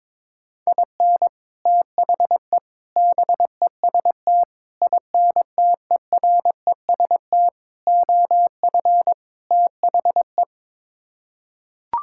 Morse code, heard as a keyed tone, about 22 wpm, 700 Hz. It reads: IN THE BEST INTEREST OF THE
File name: practice / 22wpm / Sets of 6 Words - 22wpm